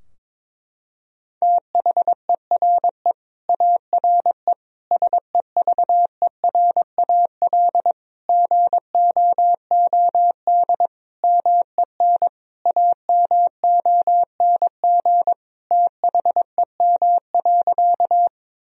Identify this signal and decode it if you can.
Morse code, heard as a keyed tone, about 22 wpm, 700 Hz. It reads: THERE ARE SEVERAL GOOD MEN AMONG THEM.
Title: Sentences from Top 1000 Words - Review Part 2 - 22wpm